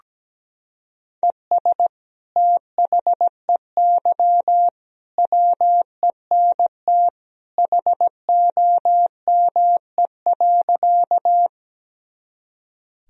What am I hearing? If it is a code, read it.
Morse code, heard as a keyed tone, about 17 wpm, 700 Hz. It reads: ES THEY WENT HOME.
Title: Sentences from Top 200 Words and Abbreviations 17wpm